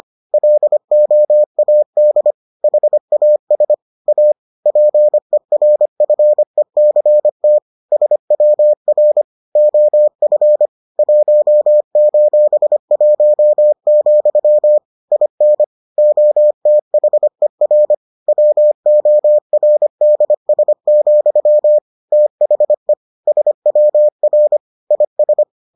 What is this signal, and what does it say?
Morse code, heard as a keyed tone, about 25 wpm, 600 Hz. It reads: LOAD HAS A PERFECT SWR OF 1:1, IN OTHER WORDS, THE SWR IS